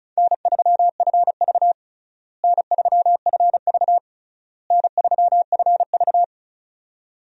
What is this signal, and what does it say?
Morse code, heard as a keyed tone, about 36 wpm, 700 Hz. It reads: N3FV N3FV N3FV
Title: USA Call Signs - 4 Characters - Repeated 3x - 35wpm